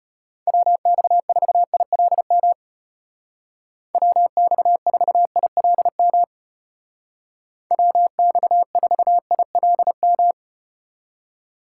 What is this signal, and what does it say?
Morse code, heard as a keyed tone, about 34 wpm, 700 Hz. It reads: WX4ILM WX4ILM WX4ILM